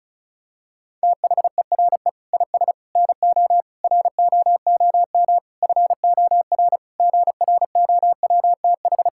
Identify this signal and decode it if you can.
Morse code, heard as a keyed tone, about 34 wpm, 700 Hz. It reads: THERE IS NO ROOM FOR GROWTH
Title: Sentences from Top 1000 Words - Review Part 5 - 35wpm